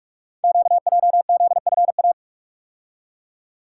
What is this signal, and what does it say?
Morse code, heard as a keyed tone, about 44 wpm, 700 Hz. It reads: QJZFA
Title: Code Groups - 5 Letters - 45wpm